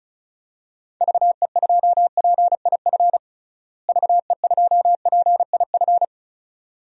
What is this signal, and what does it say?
Morse code, heard as a keyed tone, about 36 wpm, 700 Hz. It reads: VE2PIF VE2PIF
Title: DXCC-100 Call Signs - 6 Characters - Repeated 2x - 35wpm